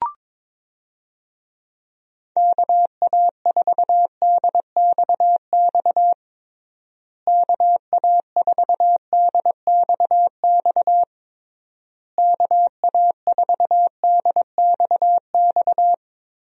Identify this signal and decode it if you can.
Morse code, heard as a keyed tone, about 21 wpm, 700 Hz. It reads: KA4DXX KA4DXX KA4DXX
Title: USA Call Signs - 6 Characters - Repeated 3x - 22wpm